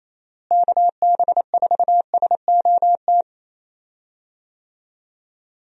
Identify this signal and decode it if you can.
Morse code, heard as a keyed tone, about 28 wpm, 700 Hz. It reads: KB4SOT
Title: USA Call Signs - 6 Characters - Repeated 2x - 28wpm